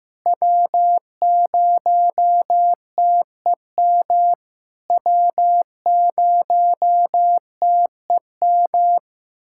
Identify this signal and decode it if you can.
Morse code, heard as a keyed tone, about 15 wpm, 700 Hz. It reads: W0TEM W0TEM
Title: SOTA QSO Practice 15wpm